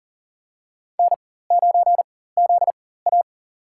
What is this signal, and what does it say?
Morse code, heard as a keyed tone, about 40 wpm, spaced as for 19 wpm, 700 Hz. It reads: N9ZA
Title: USA Call Signs - 4 Characters - 4x Character Spacing - 40wpm